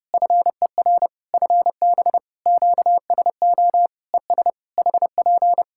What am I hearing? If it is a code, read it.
Morse code, heard as a keyed tone, about 30 wpm, 700 Hz. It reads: FER FB QSO ES HP
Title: Practice QSO - Exchange 4 - Ending - 30wpm